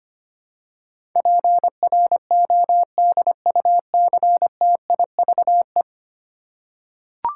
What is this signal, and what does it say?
Morse code, heard as a keyed tone, about 25 wpm, 700 Hz. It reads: PRODUCTIVE
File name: 10 Letter Words - First 3 Letters Given - 25wpm